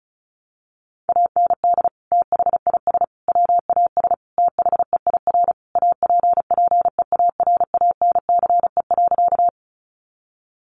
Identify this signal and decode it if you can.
Morse code, heard as a keyed tone, about 34 wpm, 700 Hz. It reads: AND THIS WAS THEIR APPEARANCE.
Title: Sentences from Top 1100 Words 35wpm